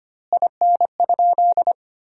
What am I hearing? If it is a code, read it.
Morse code, heard as a keyed tone, about 25 wpm, 700 Hz. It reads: IN?
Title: 2x Word-Spacing - Sentences from Top 300 Words - 25wpm